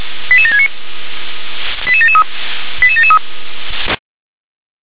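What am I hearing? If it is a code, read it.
Five-tone selective call (ZVEI), heard as quick runs of five tones, tones of about 70 ms. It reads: 88769 88773 88773
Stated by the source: ZVEI five-tone selective call received with an SDR.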